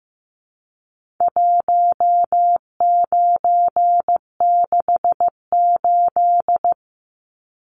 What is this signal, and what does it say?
Morse code, heard as a keyed tone, about 15 wpm, 700 Hz. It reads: 1968
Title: Numbers - 4-Digit 15wpm